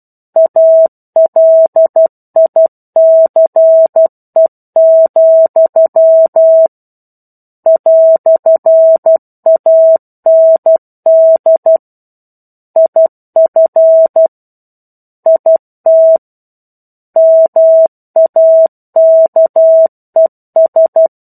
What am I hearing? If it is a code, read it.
Morse code, heard as a keyed tone, about 12 wpm, 650 Hz. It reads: ALICE, "AND IF IT MAKES
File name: 01-Chapter-01 12 WPM